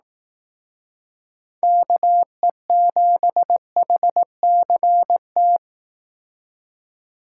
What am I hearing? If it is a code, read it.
Morse code, heard as a keyed tone, about 18 wpm, 700 Hz. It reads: KE7HCT
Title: Speed Racing - USA 2x3 Call Signs 15wpm